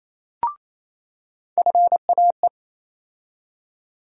Morse code, heard as a keyed tone, about 28 wpm, 700 Hz. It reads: FAE
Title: USA Call Sign Postfixes - 3 Letters 28wpm